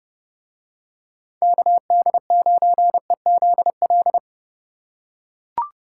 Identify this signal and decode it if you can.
Morse code, heard as a keyed tone, about 30 wpm, 700 Hz. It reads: KD9EZL